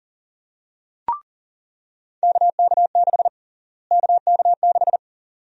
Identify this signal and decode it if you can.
Morse code, heard as a keyed tone, about 41 wpm, 700 Hz. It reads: KK6 KK6
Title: USA Call Sign Prefixes - Repeated 2x - 40wpm